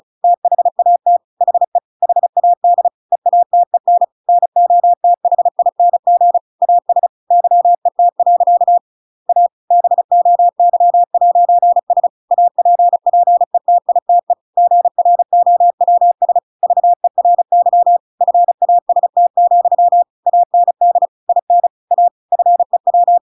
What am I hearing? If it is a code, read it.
Morse code, heard as a keyed tone, about 34 wpm, 700 Hz. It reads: THAT HE HAD EATEN NOTHING AS YET. A BOY'S APPETITE GROWS VERY FAST, AND IN A FEW
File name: Pinocchio / Pinocchio ch5 35 35 ik1hge